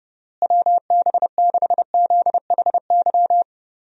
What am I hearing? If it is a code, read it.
Morse code, heard as a keyed tone, about 30 wpm, 700 Hz. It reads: WB6ZHY
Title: USA Call Signs - 6 Characters - Repeated 3x - 30wpm